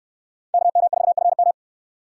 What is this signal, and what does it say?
Morse code, heard as a keyed tone, about 68 wpm, 700 Hz. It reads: KM4FM